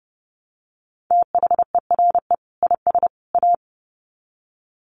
Morse code, heard as a keyed tone, about 30 wpm, 700 Hz. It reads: THERE IS A
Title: Sets of 3 Words 30wpm